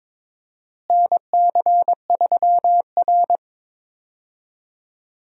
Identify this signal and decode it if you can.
Morse code, heard as a keyed tone, about 22 wpm, 700 Hz. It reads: NC3R
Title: USA Call Signs - 4 Characters - Repeated 2x - 22wpm